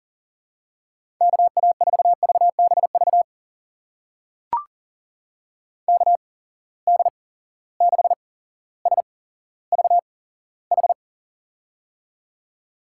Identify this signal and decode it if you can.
Morse code, heard as a keyed tone, about 40 wpm, spaced as for 17 wpm, 700 Hz. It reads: KA4VBV KD6SVH
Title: USA Call Signs - 6 Characters - 8x Character Spacing - 40wpm